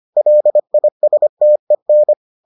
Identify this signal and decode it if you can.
Morse code, heard as a keyed tone, about 25 wpm, 600 Hz. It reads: LISTEN